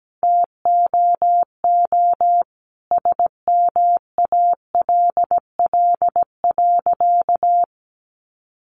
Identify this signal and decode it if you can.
Morse code, heard as a keyed tone, about 17 wpm, 700 Hz. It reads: TOO SMALL.
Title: Sentences from Top 1400 Words 17wpm